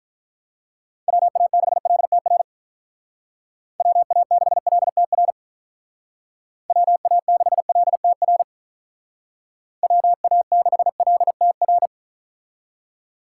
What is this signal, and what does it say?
Morse code, heard as a keyed tone, about 44 wpm, 700 Hz. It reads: WA6LTR WA6LTR WA6LTR WA6LTR
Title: Speed Racing - USA Call Signs 35wpm